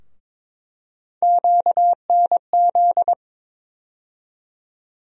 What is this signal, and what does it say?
Morse code, heard as a keyed tone, about 22 wpm, 700 Hz. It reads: QNZ
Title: USA Call Sign Postfixes - 3 Letters 22wpm